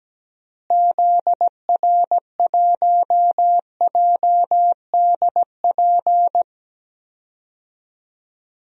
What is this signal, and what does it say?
Morse code, heard as a keyed tone, about 17 wpm, 700 Hz. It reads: ZR1JDP
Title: DXCC-100 Call Signs - 6 Characters - Repeated 2x - 17wpm